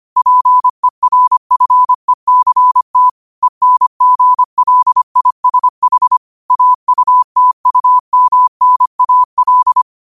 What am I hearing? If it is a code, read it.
Morse code, heard as a keyed tone, about 25 wpm, 1000 Hz. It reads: PERFECT ENGLISH AUTUMNAL
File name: qotdfiles / 25wpm